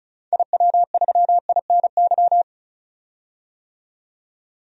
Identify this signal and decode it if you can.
Morse code, heard as a keyed tone, about 34 wpm, 700 Hz. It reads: IW3INY
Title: DXCC-100 Call Signs - 6 Characters - 35wpm